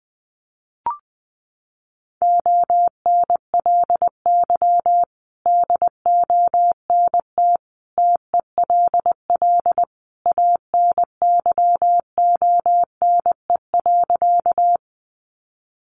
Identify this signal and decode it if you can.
Morse code, heard as a keyed tone, about 20 wpm, 700 Hz. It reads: ONLY DONT TELL ANYONE.
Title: Sentences from Top 900 Words 20wpm